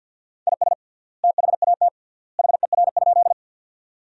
Morse code, heard as a keyed tone, about 50 wpm, 700 Hz. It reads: IS THAT HER?
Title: 3x Word-Spacing - Sentences from Top 100 Words - 50wpm